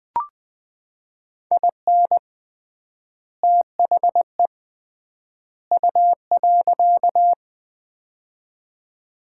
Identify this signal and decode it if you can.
Morse code, heard as a keyed tone, about 20 wpm, 700 Hz. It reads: IN THE U.
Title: 3x Word-Spacing - Sets of 3 Words - 20wpm